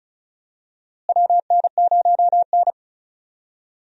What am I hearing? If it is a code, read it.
Morse code, heard as a keyed tone, about 36 wpm, 700 Hz. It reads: WN0N
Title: USA Call Signs - 4 Characters - Repeated 2x - 35wpm